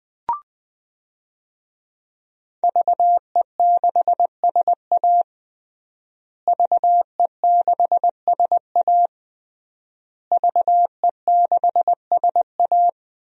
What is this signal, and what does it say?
Morse code, heard as a keyed tone, about 20 wpm, 700 Hz. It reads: VE6SA VE6SA VE6SA
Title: DXCC-100 Call Signs - 5 Characters - Repeated 3x - 20wpm